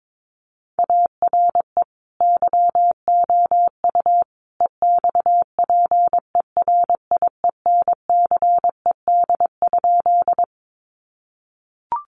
Morse code, heard as a keyed tone, about 22 wpm, 700 Hz. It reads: ARE YOU EXPERIENCED?
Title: Sentences from Top 1100 Words 22wpm